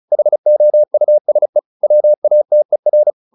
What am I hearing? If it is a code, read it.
Morse code, heard as a keyed tone, about 35 wpm, 600 Hz. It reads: HOUSE WATER